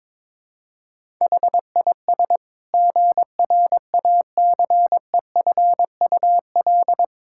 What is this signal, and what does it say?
Morse code, heard as a keyed tone, about 22 wpm, 700 Hz. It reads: HIS GRACEFUL
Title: Pangram Challenge v2 - 22wpm